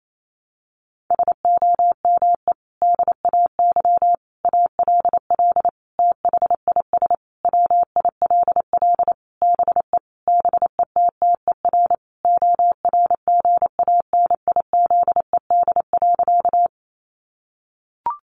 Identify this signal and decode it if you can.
Morse code, heard as a keyed tone, about 28 wpm, 700 Hz. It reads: SOME DAY ALL THIS WILL BE BETTER ORGANIZED.
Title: Sentences from Top 1300 Words 28wpm